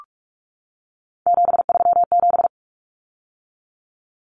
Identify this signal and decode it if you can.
Morse code, heard as a keyed tone, about 44 wpm, 700 Hz. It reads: 737
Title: Common Cut Numbers - 3-Digit 45wpm